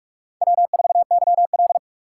Morse code, heard as a keyed tone, about 45 wpm, 700 Hz. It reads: W4YL